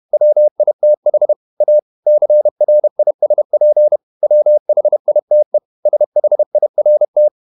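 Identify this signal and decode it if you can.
Morse code, heard as a keyed tone, about 31 wpm, 600 Hz. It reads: WITH A CRISP WHITE SHIRT
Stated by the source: Morse code at 30 wpm, the novel Never by Ken Follett in English, from CW med Gnister.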